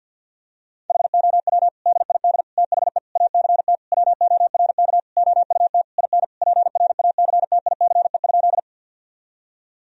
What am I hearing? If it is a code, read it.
Morse code, heard as a keyed tone, about 50 wpm, 700 Hz. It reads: HOW DID THE ACT WORK OUT IN PRACTICE?
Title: Sentences from Top 1000 Words - Review Part 4 - 50wpm